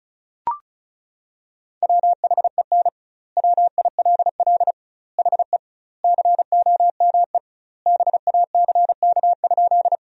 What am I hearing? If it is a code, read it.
Morse code, heard as a keyed tone, about 34 wpm, 700 Hz. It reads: WHEN WILL HE COME BACK?
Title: 2x Word-Spacing - Sentences from Top 200 Words - 35wpm